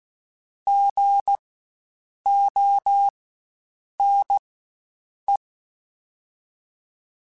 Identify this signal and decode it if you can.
Morse code, heard as a keyed tone, about 16 wpm, spaced as for 7 wpm, 800 Hz. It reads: GONE
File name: Spaced Alphas / Spaced Alphas mp3s / Gs sp @15